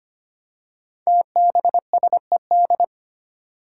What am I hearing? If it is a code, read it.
Morse code, heard as a keyed tone, about 25 wpm, 700 Hz. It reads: TBSED
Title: Five Challenging Letter-Number Combinations 25wpm